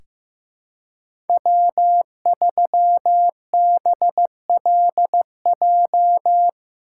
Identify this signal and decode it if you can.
Morse code, heard as a keyed tone, about 15 wpm, 700 Hz. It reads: W3BLJ
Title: USA Call Signs - 5 Characters - 6x Character Spacing - 15wpm